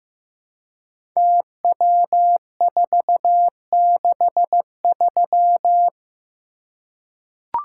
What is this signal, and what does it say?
Morse code, heard as a keyed tone, about 15 wpm, 700 Hz. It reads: TW463
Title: Code Groups - 5 Letter:Numbers - 15wpm